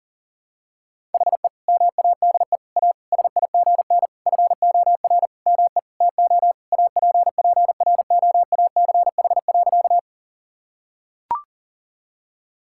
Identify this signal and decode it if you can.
Morse code, heard as a keyed tone, about 40 wpm, 700 Hz. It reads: HE MADE A SIGN FOR ME TO APPROACH.
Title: Sentences from Top 1000 Words - Review Part 6 - 40wpm